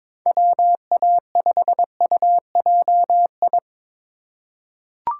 Morse code, heard as a keyed tone, about 22 wpm, 700 Hz. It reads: WA5UJI
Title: USA Call Signs - 3x Character Spacing - 22wpm